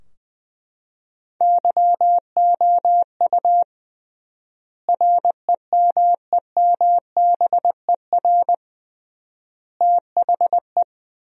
Morse code, heard as a keyed tone, about 20 wpm, 700 Hz. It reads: YOU REMEMBER THE
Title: 3x Word-Spacing - Sentences from Top 400 Words - 20wpm